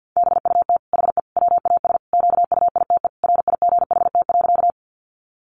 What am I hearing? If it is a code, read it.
Morse code, heard as a keyed tone, about 50 wpm, 700 Hz. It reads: BUT HE WAS QUITE RIGHT.
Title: Sentences from Top 700 Words 50wpm